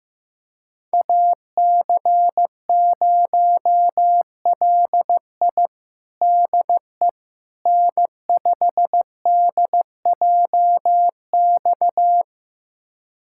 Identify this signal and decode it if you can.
Morse code, heard as a keyed tone, about 15 wpm, 700 Hz. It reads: AC0LI DE N5DJX
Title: USA Callsign DE Callsign 15wpm